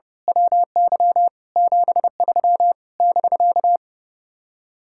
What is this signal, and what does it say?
Morse code, heard as a keyed tone, about 28 wpm, 700 Hz. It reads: WY 73 <BK>
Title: Speed-Racing Common QSO Phrases - 30wpm